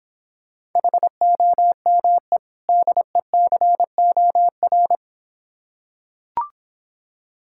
Morse code, heard as a keyed tone, about 26 wpm, 700 Hz. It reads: HOME DECOR